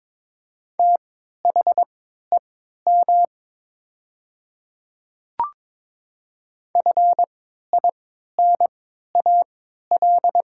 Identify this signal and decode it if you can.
Morse code, heard as a keyed tone, about 22 wpm, spaced as for 12 wpm, 700 Hz. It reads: THEM FINAL
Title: 3x Character Spacing - Top 500 Words - 22wpm